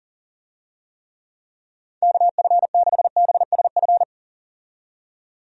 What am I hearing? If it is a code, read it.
Morse code, heard as a keyed tone, about 40 wpm, 700 Hz. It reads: KF6BSF
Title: USA Call Signs - 6 Characters - 40wpm